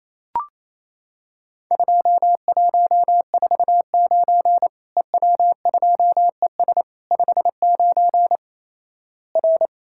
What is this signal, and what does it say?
Morse code, heard as a keyed tone, about 28 wpm, 700 Hz. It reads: 2149 EW2ES 59 R
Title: Worked All Europe DX Contest - QTC Exchange - 28wpm